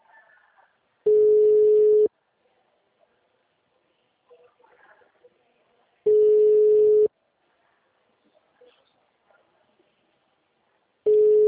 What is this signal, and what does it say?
Ringback tone (Europe, 425 Hz).